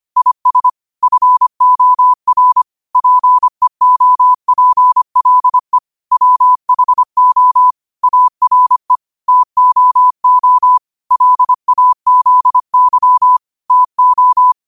Morse code, heard as a keyed tone, about 25 wpm, 1000 Hz. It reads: IS FOR PEOPLE WHO ARE TOO LAZY TO